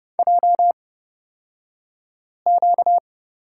Morse code, heard as a keyed tone, about 30 wpm, spaced as for 5 wpm, 700 Hz. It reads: JQ
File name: Lesson 38d - Letter q - Two Characters 30wpm